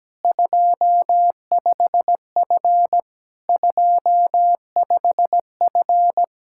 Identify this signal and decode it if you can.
Morse code, heard as a keyed tone, about 17 wpm, 700 Hz. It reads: 25F 25F
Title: Speed-Racing Common QSO Phrases Encore - 17wpm